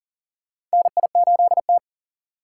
Morse code, heard as a keyed tone, about 40 wpm, 700 Hz. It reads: NI8T
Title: USA Call Signs - 4 Characters - 4x Character Spacing - 40wpm